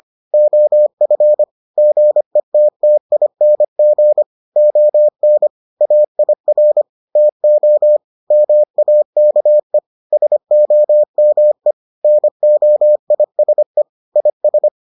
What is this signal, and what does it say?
Morse code, heard as a keyed tone, about 25 wpm, 600 Hz. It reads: OF GETTING ON AIR TO MAKE SOME NOISE IS